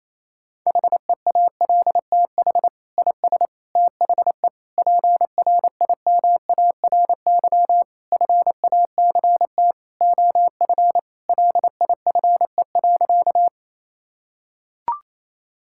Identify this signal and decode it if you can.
Morse code, heard as a keyed tone, about 28 wpm, 700 Hz. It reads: HEALTH IS THE PRIMARY FACT OF LIFE.
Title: Sentences from Top 1000 Words - Review Part 5 - 28wpm